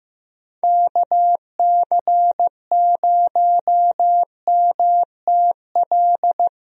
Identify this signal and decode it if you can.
Morse code, heard as a keyed tone, about 15 wpm, 700 Hz. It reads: KC0MTL
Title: USA Call Signs - 6 Characters - 15wpm